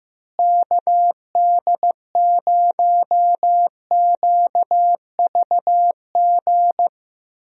Morse code, heard as a keyed tone, about 16 wpm, 700 Hz. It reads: KD0QVG
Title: USA Call Signs - 6 Characters - Repeated 3x - 15wpm